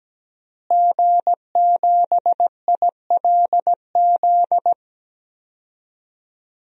Morse code, heard as a keyed tone, about 17 wpm, 700 Hz. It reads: G7ILZ
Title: DXCC-100 Call Signs - 5 Characters - Repeated 3x - 17wpm